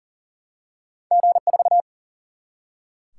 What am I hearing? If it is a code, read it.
Morse code, heard as a keyed tone, about 38 wpm, 700 Hz. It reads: G4
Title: Single-Letter Single-Number 40wpm